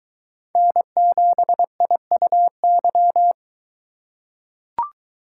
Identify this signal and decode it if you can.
Morse code, heard as a keyed tone, about 22 wpm, 700 Hz. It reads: N7IUY